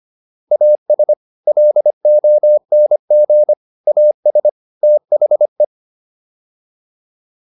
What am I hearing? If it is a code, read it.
Morse code, heard as a keyed tone, about 25 wpm, 600 Hz. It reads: AS LONG AS THE